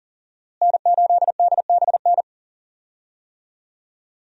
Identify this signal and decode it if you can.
Morse code, heard as a keyed tone, about 40 wpm, 700 Hz. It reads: N8DBN
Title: USA Call Signs - 5 Characters - Repeated 2x - 40wpm